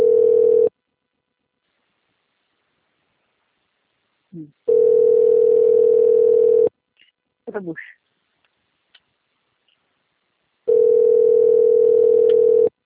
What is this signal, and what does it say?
Ringback tone (North America), 440 + 480 Hz, with speech.
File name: rg-1600-unknown- - - .1227